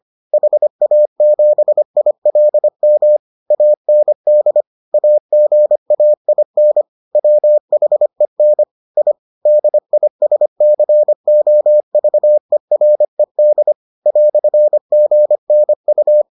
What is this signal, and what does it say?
Morse code, heard as a keyed tone, about 25 wpm, 600 Hz. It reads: HA7ILM AND AGAIN WHEN I DISCOVERED "GNU